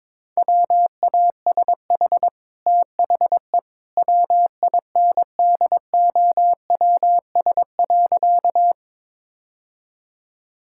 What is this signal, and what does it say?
Morse code, heard as a keyed tone, about 22 wpm, 690 Hz. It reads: WASH THE WINDOWS.
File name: practice / 22wpm / Easy Sentences with Hints - 22wpm